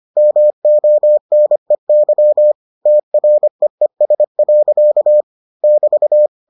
Morse code, heard as a keyed tone, about 25 wpm, 600 Hz. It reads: MONEY TREES. <BT>